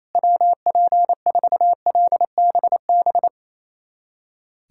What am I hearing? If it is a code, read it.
Morse code, heard as a keyed tone, about 28 wpm, 700 Hz. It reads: WP4LBB
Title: USA Callsign DE Callsign - 6 Characters - 28wpm